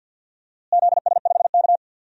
Morse code, heard as a keyed tone, about 49 wpm, 700 Hz. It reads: ZS5K
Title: DXCC-100 Call Signs - 4 Characters - Repeated 3x - 50wpm